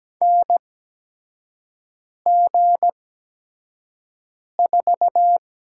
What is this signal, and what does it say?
Morse code, heard as a keyed tone, about 17 wpm, spaced as for 5 wpm, 700 Hz. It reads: NG4
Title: USA Call Signs - 4 Characters - 8x Character Spacing - 17wpm